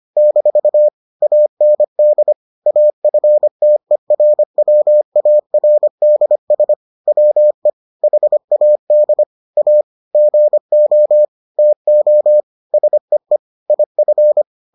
Morse code, heard as a keyed tone, about 25 wpm, 600 Hz. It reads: - AND AFTERWARDS WE HAD A GO TO SEE IF